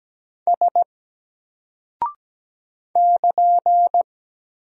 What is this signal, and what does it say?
Morse code, heard as a keyed tone, about 17 wpm, 700 Hz. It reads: S <KN>
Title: Single Character (Letter:Number:Punctuation:Prosign) 17wpm